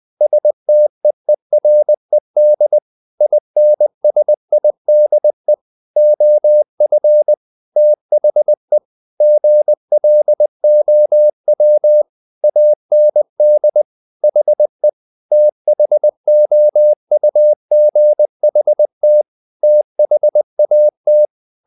Morse code, heard as a keyed tone, about 20 wpm, 600 Hz. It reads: STEERED INSIDE OF THE GLOW AND HE THOUGHT THAT
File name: the old man and the sea part 2 20wpm